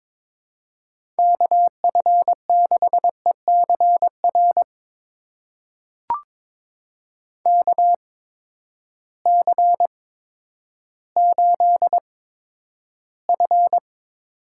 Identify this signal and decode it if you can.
Morse code, heard as a keyed tone, about 22 wpm, spaced as for 11 wpm, 700 Hz. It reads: KF6ECR KC8F
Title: USA Call Signs - 6 Characters - 8x Character Spacing - 22wpm